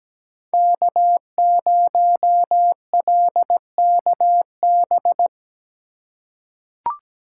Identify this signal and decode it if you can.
Morse code, heard as a keyed tone, about 17 wpm, 700 Hz. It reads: K0LKB
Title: USA Call Signs - 5 Characters - 3x Character Spacing - 17wpm